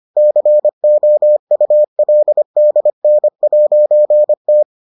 Morse code, heard as a keyed tone, about 25 wpm, 600 Hz. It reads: COULDN'T